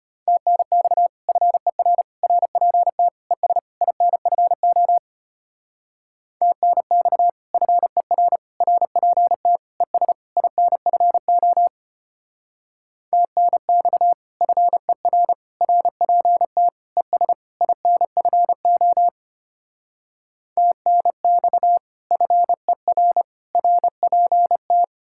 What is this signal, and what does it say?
Morse code, heard as a keyed tone, about 31 wpm, 700 Hz. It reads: TNX FER RPT ES INFO TNX FER RPT ES INFO TNX FER RPT ES INFO TNX FER RPT
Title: Speed-Racing Common QSO Phrases - 25wpm